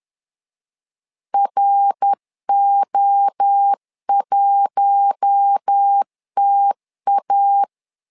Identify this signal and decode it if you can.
Morse code, heard as a keyed tone, about 10 wpm, 800 Hz. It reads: RO1TA
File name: LEZIONI / 4 Q1 10wpm 2448